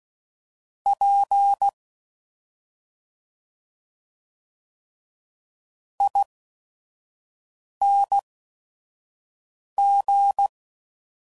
Morse code, heard as a keyed tone, about 16 wpm, spaced as for 5 wpm, 800 Hz. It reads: P ING